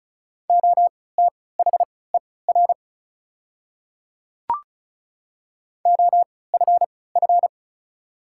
Morse code, heard as a keyed tone, about 36 wpm, spaced as for 20 wpm, 700 Hz. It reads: OTHER OFF
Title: 3x Character Spacing - Top 200 Words - 35wpm